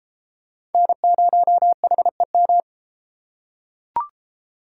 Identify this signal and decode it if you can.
Morse code, heard as a keyed tone, about 33 wpm, 700 Hz. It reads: N0HEM